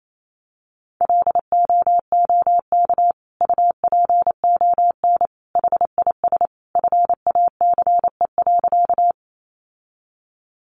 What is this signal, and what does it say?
Morse code, heard as a keyed tone, about 28 wpm, 700 Hz. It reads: LOOK UPON HIS FACE.
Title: Sentences from Top 1100 Words 28wpm